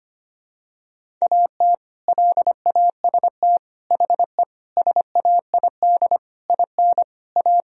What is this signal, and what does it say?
Morse code, heard as a keyed tone, about 25 wpm, 700 Hz. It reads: AT LAST HE SAID IN A
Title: Sentences from Top 1000 Words - Review Part 1 - 25wpm